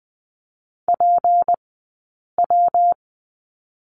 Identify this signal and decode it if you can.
Morse code, heard as a keyed tone, about 20 wpm, 700 Hz. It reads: P W
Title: 158 Lesson 26d - Letter g - Two Characters 20wpm